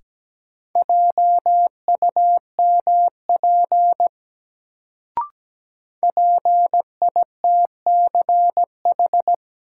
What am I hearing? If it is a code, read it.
Morse code, heard as a keyed tone, about 17 wpm, 700 Hz. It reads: JUMP PITCH
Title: Sentences from Top 1300 Words - New Word Drill Alt - 17wpm